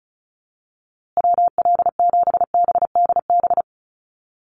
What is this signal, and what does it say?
Morse code, heard as a keyed tone, about 34 wpm, 700 Hz. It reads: WL7BDB
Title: USA Call Signs Repeated 2x 35wpm